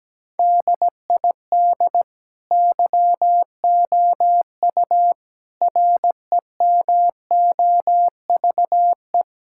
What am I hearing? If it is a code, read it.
Morse code, heard as a keyed tone, about 17 wpm, 700 Hz. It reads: DID YOU REMOVE
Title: Sentences from Top 1000 Words - Review Part 5 - 17wpm